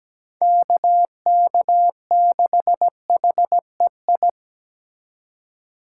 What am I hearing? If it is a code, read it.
Morse code, heard as a keyed tone, about 20 wpm, 700 Hz. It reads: KK6HEI
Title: Speed Racing - USA Call Signs 17wpm